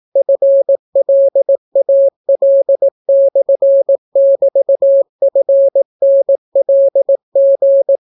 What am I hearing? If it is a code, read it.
Morse code, heard as a keyed tone, about 18 wpm, 550 Hz. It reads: FLAL/<BT>FNLG